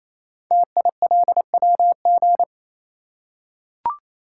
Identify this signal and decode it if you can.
Morse code, heard as a keyed tone, about 27 wpm, 700 Hz. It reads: TILWG
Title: Code Groups - 5 Letters - 28wpm